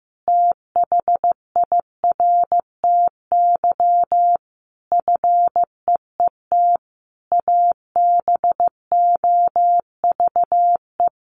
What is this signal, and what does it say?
Morse code, heard as a keyed tone, about 15 wpm, 700 Hz. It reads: THIRTY FEET ABOVE
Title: A Sound Of Thunder 15wpm